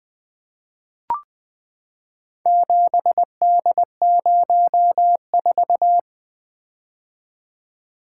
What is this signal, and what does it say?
Morse code, heard as a keyed tone, about 21 wpm, 700 Hz. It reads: 7D04
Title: Four Letter-Number Combinations - 20wpm